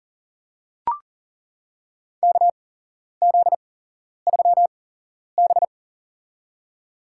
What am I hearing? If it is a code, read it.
Morse code, heard as a keyed tone, about 40 wpm, spaced as for 11 wpm, 700 Hz. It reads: KZ3B
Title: USA Call Signs - 4 Characters - 8x Character Spacing - 40wpm